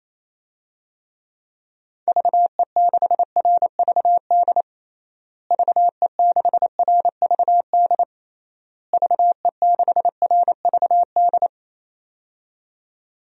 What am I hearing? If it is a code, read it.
Morse code, heard as a keyed tone, about 28 wpm, 700 Hz. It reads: VE6RVD VE6RVD VE6RVD
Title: DXCC-100 Call Signs - 6 Characters - Repeated 3x - 28wpm